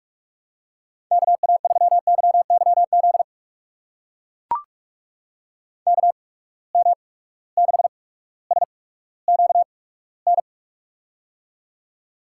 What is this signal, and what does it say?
Morse code, heard as a keyed tone, about 44 wpm, spaced as for 19 wpm, 700 Hz. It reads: KA3YYZ KM6SQN
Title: USA Call Signs - 6 Characters - 8x Character Spacing - 45wpm